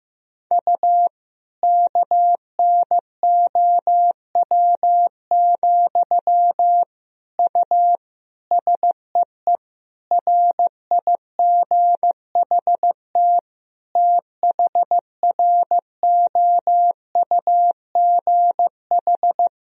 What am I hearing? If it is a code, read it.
Morse code, heard as a keyed tone, about 15 wpm, 700 Hz. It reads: U KNOW, U SEE RIGHT THROUGH
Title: Sentences from Top 200 Words and Abbreviations 15wpm